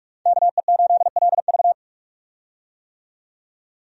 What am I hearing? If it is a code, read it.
Morse code, heard as a keyed tone, about 43 wpm, 700 Hz. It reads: KE8LV
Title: USA Call Signs - 5 Characters - 45wpm